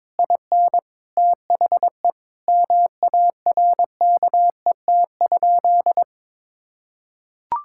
Morse code, heard as a keyed tone, about 22 wpm, 700 Hz. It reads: IN THE MARKET?
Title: Sentences from Top 1000 Words - Review Part 6 - 22wpm